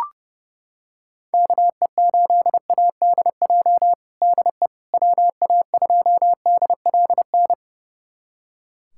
Morse code, heard as a keyed tone, about 32 wpm, 700 Hz. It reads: KE8ADJ DE WA2DLN
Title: USA Callsign DE Callsign - 6 Characters - 30wpm